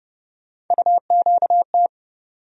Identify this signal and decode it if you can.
Morse code, heard as a keyed tone, about 30 wpm, 700 Hz. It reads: UQT
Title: USA Call Sign Postfixes - 3 Letters 30wpm